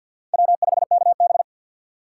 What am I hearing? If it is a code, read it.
Morse code, heard as a keyed tone, about 48 wpm, 700 Hz. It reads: W5KB
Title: USA Call Signs - 4 Characters - 3x Character Spacing - 50wpm